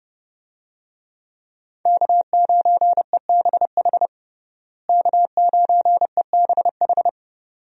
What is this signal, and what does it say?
Morse code, heard as a keyed tone, about 29 wpm, 700 Hz. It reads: K9EBH K9EBH
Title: USA Call Signs - 5 Characters - Repeated 2x - 30wpm